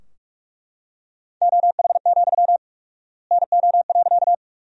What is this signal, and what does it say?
Morse code, heard as a keyed tone, about 42 wpm, 700 Hz. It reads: OH, NO.
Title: 4x Word-Spacing - Sentences from Top 500 Words - 45wpm